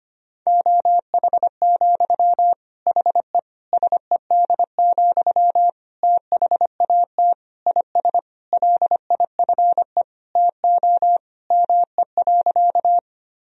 Morse code, heard as a keyed tone, about 25 wpm, 700 Hz. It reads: OH, HE SED, THAT IS LIFE TO ME.
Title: Sentences from Top 500 Words and Abbreviations 25wpm